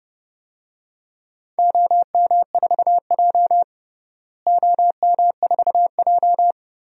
Morse code, heard as a keyed tone, about 30 wpm, 700 Hz. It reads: OM4J OM4J
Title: DXCC-100 Call Signs - 4 Characters - Repeated 3x - 30wpm